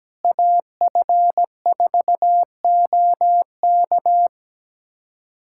Morse code, heard as a keyed tone, about 17 wpm, 700 Hz. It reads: AF4OK
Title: USA Callsign DE Callsign - 5 Characters - 17wpm